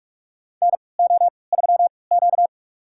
Morse code, heard as a keyed tone, about 46 wpm, spaced as for 25 wpm, 700 Hz. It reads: NO3Q
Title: USA Call Signs - 4 Characters - 3x Character Spacing - 45wpm